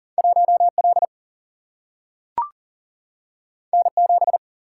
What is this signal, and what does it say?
Morse code, heard as a keyed tone, about 40 wpm, 700 Hz. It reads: 1L N7
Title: Single-Letter Single-Number 40wpm